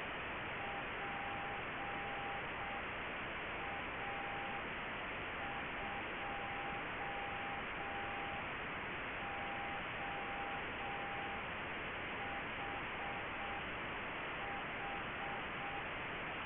Morse code, heard as a keyed tone, about 6 wpm, 800 Hz, with radio noise. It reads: PA2OHH